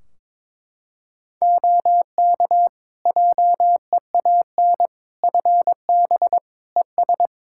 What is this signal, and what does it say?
Morse code, heard as a keyed tone, about 22 wpm, 700 Hz. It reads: OK JEAN FB ES
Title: Practice QSO - Exchange 2 - Info - 22wpm